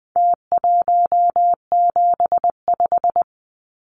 Morse code, heard as a keyed tone, about 20 wpm, 700 Hz. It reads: T175
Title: Numbers - Contest Style - 4-Digit 20wpm